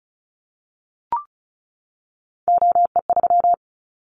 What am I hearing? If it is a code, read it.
Morse code, heard as a keyed tone, about 36 wpm, 700 Hz. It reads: OE3
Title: Callsign Prefixes for Top 100 DXCC - 35wpm